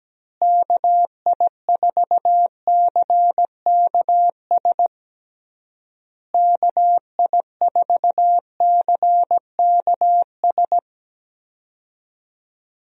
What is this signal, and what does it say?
Morse code, heard as a keyed tone, about 17 wpm, 700 Hz. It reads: KI4CKS KI4CKS
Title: USA Call Signs - 6 Characters - Repeated 3x - 17wpm